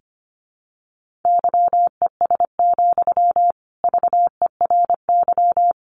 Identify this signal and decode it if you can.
Morse code, heard as a keyed tone, about 25 wpm, 700 Hz. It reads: YES, VERY
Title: Sentences from Top 600 Words 25wpm